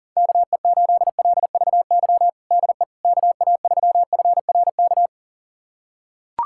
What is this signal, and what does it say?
Morse code, heard as a keyed tone, about 40 wpm, 700 Hz. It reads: KE8LVY DE KA3FRK
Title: USA Callsign DE Callsign - 6 Characters - 40wpm